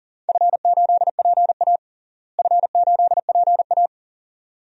Morse code, heard as a keyed tone, about 40 wpm, 700 Hz. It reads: F8PA F8PA
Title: DXCC-100 Call Signs - 4 Characters - Repeated 2x - 40wpm